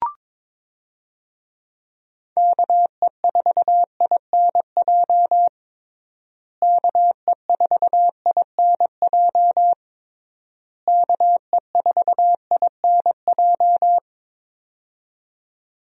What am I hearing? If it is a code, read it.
Morse code, heard as a keyed tone, about 22 wpm, 700 Hz. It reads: KE4INJ KE4INJ KE4INJ
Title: USA Call Signs - 6 Characters - Repeated 3x - 22wpm